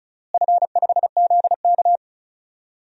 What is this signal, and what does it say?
Morse code, heard as a keyed tone, about 36 wpm, 700 Hz. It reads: F5ZK
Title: DXCC-100 Call Signs - 4 Characters - Repeated 2x - 35wpm